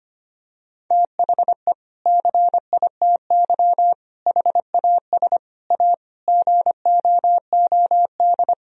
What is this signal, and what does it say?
Morse code, heard as a keyed tone, about 25 wpm, 700 Hz. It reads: THE CITY HAS A GOOD
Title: Sentences from Top 1000 Words - Review Part 4 - 25wpm